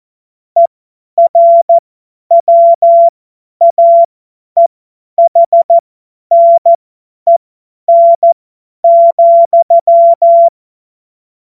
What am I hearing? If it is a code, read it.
Morse code, heard as a keyed tone, about 14 wpm, spaced as for 10 wpm, 680 Hz. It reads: ERWAEHNEN,
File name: Morsekurs / Witze / Witz 17. Mai 2022 50 F70